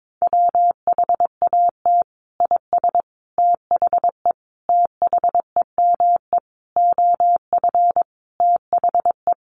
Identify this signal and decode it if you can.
Morse code, heard as a keyed tone, about 22 wpm, 700 Hz. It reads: WHAT IS THE THEME OF THE
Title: Sentences from Top 1100 Words 22wpm